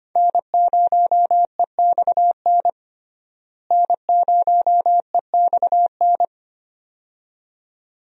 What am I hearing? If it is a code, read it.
Morse code, heard as a keyed tone, about 25 wpm, 700 Hz. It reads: N0EXN N0EXN
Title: USA Call Signs - 5 Characters - Repeated 2x - 25wpm